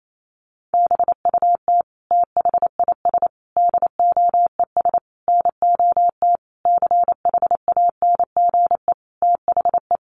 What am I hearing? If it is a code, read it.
Morse code, heard as a keyed tone, about 28 wpm, 700 Hz. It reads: BUT THIS DOES NOT CHANGE THE
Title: Sentences from Top 1400 Words 28wpm